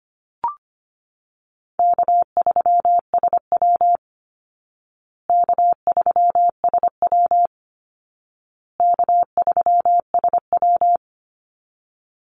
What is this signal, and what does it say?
Morse code, heard as a keyed tone, about 25 wpm, 700 Hz. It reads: K3SW K3SW K3SW
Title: Easy Call Signs Repeated 3x - 25wpm